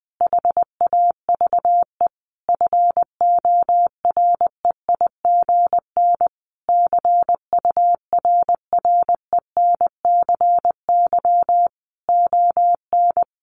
Morse code, heard as a keyed tone, about 20 wpm, 700 Hz. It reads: HAVE FOREIGN CURRENCY ON
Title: Sentences from Top 1500 Words 20wpm